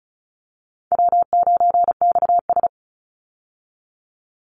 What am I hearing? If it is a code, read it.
Morse code, heard as a keyed tone, about 34 wpm, 700 Hz. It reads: W9XS
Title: Easy Call Signs Repeated 2x - 35wpm